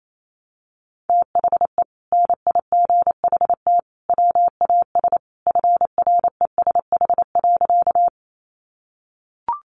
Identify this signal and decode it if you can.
Morse code, heard as a keyed tone, about 28 wpm, 700 Hz. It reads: THE NIGHT WAS FRESH.
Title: Sentences from Top 1200 Words 28wpm